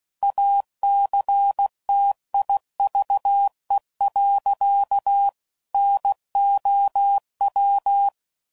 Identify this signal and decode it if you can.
Morse code, heard as a keyed tone, about 16 wpm, 800 Hz. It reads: ACTIVE. NOW
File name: CW Code Groups @15WPM / Technical Subjects / AND Gate